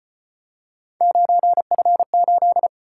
Morse code, heard as a keyed tone, about 34 wpm, 700 Hz. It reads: 9F8